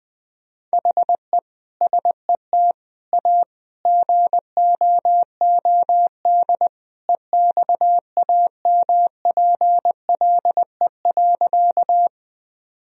Morse code, heard as a keyed tone, about 20 wpm, 700 Hz. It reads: HE SET A GOOD EXAMPLE.
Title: Sentences from Top 1000 Words - Review Part 1 - 20wpm